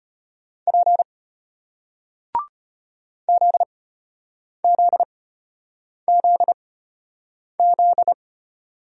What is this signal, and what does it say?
Morse code, heard as a keyed tone, about 33 wpm, spaced as for 8 wpm, 700 Hz. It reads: P ZZZZ